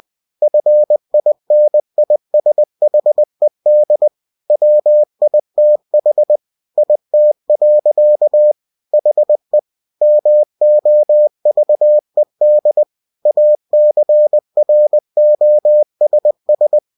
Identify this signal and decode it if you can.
Morse code, heard as a keyed tone, about 20 wpm, 600 Hz. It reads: FINISHED WITH IT. HE MOVED ACROSS
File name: cw-books / winnie-the-pooh / winnie-20wpm-page-0084